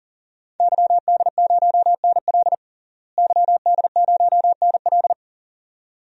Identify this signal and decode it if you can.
Morse code, heard as a keyed tone, about 40 wpm, 700 Hz. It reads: YD0NL YD0NL
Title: DXCC-100 Call Signs - 5 Characters - Repeated 2x - 40wpm